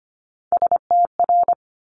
Morse code, heard as a keyed tone, about 25 wpm, 700 Hz. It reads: STR
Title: Multi-letter Phonemes Quick Fire 25wpm